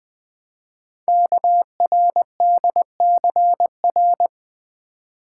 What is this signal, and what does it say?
Morse code, heard as a keyed tone, about 20 wpm, 700 Hz. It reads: KRDCR
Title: Code Groups - 5 Letters - 20wpm